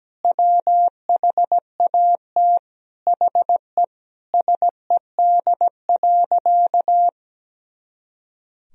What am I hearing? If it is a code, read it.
Morse code, heard as a keyed tone, about 17 wpm, 700 Hz. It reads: WHAT HE SED.
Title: Sentences from Top 400 Words and Abbreviations 17wpm